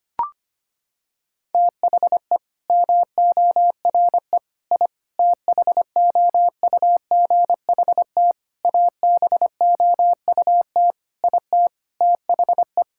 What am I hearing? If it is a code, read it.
Morse code, heard as a keyed tone, about 25 wpm, 700 Hz. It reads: THE MORE I THOUGHT ABOUT IT THE
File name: Sets of 9 Words - 25wpm